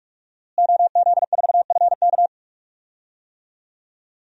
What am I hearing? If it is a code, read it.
Morse code, heard as a keyed tone, about 42 wpm, 700 Hz. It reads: OZ4FK
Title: DXCC-100 Call Signs - 5 Characters - Repeated 2x - 45wpm